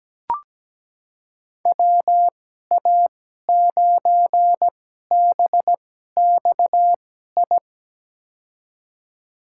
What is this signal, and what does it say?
Morse code, heard as a keyed tone, about 17 wpm, spaced as for 12 wpm, 700 Hz. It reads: WA9BXI
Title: USA Call Signs - 2x Character Spacing - 17wpm